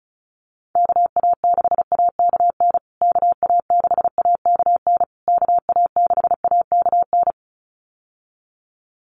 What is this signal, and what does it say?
Morse code, heard as a keyed tone, about 34 wpm, 700 Hz. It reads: KA6AKN KA6AKN KA6AKN
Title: USA Call Signs Repeated 3x 35wpm